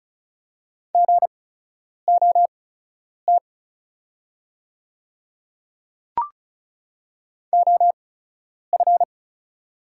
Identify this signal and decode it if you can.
Morse code, heard as a keyed tone, about 36 wpm, spaced as for 10 wpm, 700 Hz. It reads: GOT OF
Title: 8x Character Spacing - Top 300 Words - 35wpm